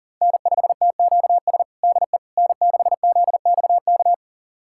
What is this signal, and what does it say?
Morse code, heard as a keyed tone, about 40 wpm, 700 Hz. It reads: N5TQS DE N6ZXK
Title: USA Callsign DE Callsign - 5 Characters - 40wpm